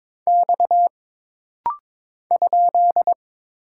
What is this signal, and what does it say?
Morse code, heard as a keyed tone, about 22 wpm, 700 Hz. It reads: X ?
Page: This rapid-fire timing is geared to push your ICR (Instant Character Recognition) skills.